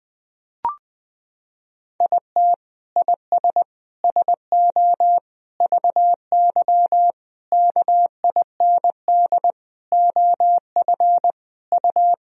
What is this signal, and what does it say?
Morse code, heard as a keyed tone, about 20 wpm, 700 Hz. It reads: IT IS SO VY KIND OF U
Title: Sentences from Top 300 Words and Abbreviations 20wpm